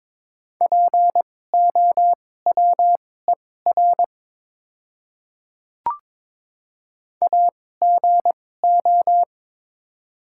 Morse code, heard as a keyed tone, about 22 wpm, spaced as for 16 wpm, 700 Hz. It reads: POWER AGO
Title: 2x Character Spacing - Top 500 Words - 22wpm